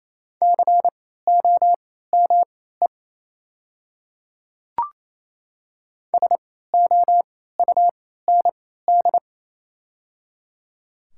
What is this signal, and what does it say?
Morse code, heard as a keyed tone, about 27 wpm, spaced as for 16 wpm, 700 Hz. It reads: COME SOUND
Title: 3x Character Spacing - Top 100 Words - 28wpm